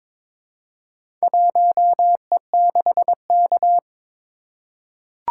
Morse code, heard as a keyed tone, about 22 wpm, 700 Hz. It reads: 1E6K
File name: practice / 22wpm / Four Letter-Number Combinations - 22wpm